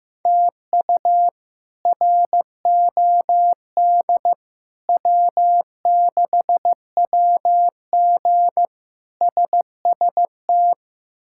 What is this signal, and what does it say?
Morse code, heard as a keyed tone, about 15 wpm, 700 Hz. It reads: TU ROD W6WG SST
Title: K1USN SST QSO Practice - 15wpm